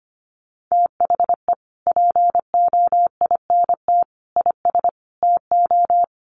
Morse code, heard as a keyed tone, about 25 wpm, 700 Hz. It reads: THE POINT IS TO
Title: Sentences from Top 1300 Words 25wpm